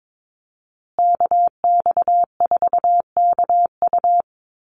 Morse code, heard as a keyed tone, about 22 wpm, 690 Hz. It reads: KX4KU
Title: Call Signs Repeated 2x - 22wpm